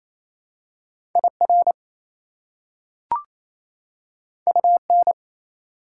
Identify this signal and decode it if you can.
Morse code, heard as a keyed tone, about 28 wpm, 700 Hz. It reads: IR UN
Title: Prefixes and Suffixes 28wpm